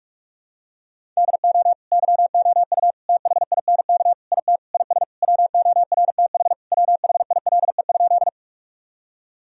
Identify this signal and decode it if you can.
Morse code, heard as a keyed tone, about 41 wpm, 700 Hz. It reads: DO YOU THINK IT IS WORTH WHILE?
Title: Sentences from Top 1000 Words - Review Part 1 - 45wpm